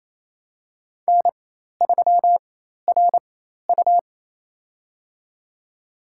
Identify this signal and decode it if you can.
Morse code, heard as a keyed tone, about 28 wpm, spaced as for 13 wpm, 700 Hz. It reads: N3RU
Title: USA Call Signs - 4 Characters - 4x Character Spacing - 28wpm